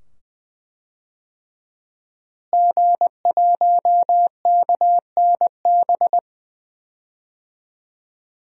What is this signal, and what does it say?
Morse code, heard as a keyed tone, about 21 wpm, 700 Hz. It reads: G1KNB
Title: DXCC-100 Call Signs - 5 Characters - Repeated 2x - 20wpm